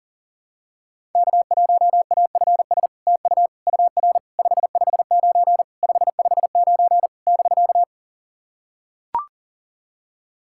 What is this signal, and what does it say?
Morse code, heard as a keyed tone, about 40 wpm, 700 Hz. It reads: K1AFS TU UR 559 559 <BK>
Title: SOTA QSO Practice 40wpm